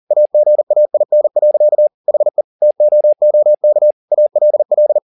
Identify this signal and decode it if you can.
Morse code, heard as a keyed tone, about 40 wpm, 600 Hz. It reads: AGAIN. HE TOOK ALL
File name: the old man and the sea 2 40wpm